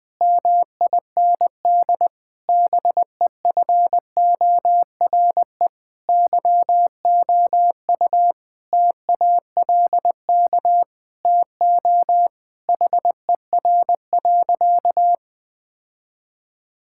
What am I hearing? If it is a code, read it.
Morse code, heard as a keyed tone, about 20 wpm, 700 Hz. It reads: MIND BEFORE YOU TALK TO HER.
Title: Sentences from Top 1000 Words - Review Part 6 - 20wpm